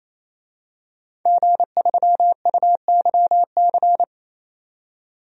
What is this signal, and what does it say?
Morse code, heard as a keyed tone, about 28 wpm, 700 Hz. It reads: G3UYC
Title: DXCC-100 Call Signs - 5 Characters - 28wpm